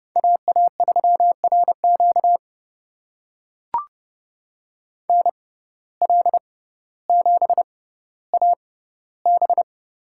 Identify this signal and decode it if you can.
Morse code, heard as a keyed tone, about 30 wpm, spaced as for 15 wpm, 700 Hz. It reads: AA3RQ NL7AB
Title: USA Call Signs - 6x Character Spacing - 30wpm